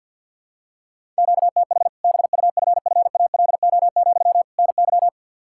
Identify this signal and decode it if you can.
Morse code, heard as a keyed tone, about 47 wpm, 700 Hz. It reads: QTH BUFFALO, NY
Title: Practice QSO - Exchange 1 - Beginning - 50wpm